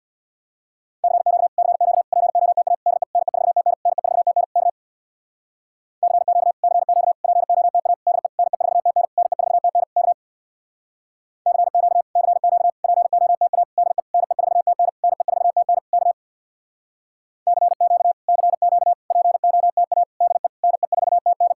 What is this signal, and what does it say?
Morse code, heard as a keyed tone, about 63 wpm, 700 Hz. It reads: CQ CQ POTA DE NE4TN NE4TN K CQ CQ POTA DE NE4TN NE4TN K CQ CQ POTA DE NE4TN NE4TN K CQ CQ POTA DE NE4TN